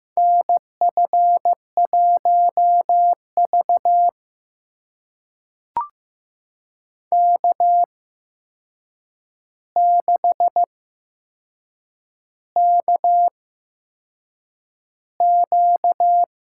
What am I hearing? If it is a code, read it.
Morse code, heard as a keyed tone, about 15 wpm, spaced as for 6 wpm, 700 Hz. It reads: NF1V K6KQ
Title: USA Call Signs - 4 Characters - 8x Character Spacing - 15wpm